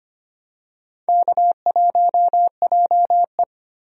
Morse code, heard as a keyed tone, about 25 wpm, 700 Hz. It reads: K1JE
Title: USA Call Signs - 4 Characters - Repeated 3x - 25wpm